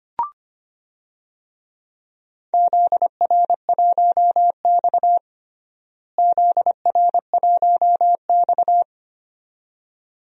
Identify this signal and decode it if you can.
Morse code, heard as a keyed tone, about 25 wpm, 700 Hz. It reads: ZR1X ZR1X
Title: DXCC-100 Call Signs - 4 Characters - Repeated 2x - 25wpm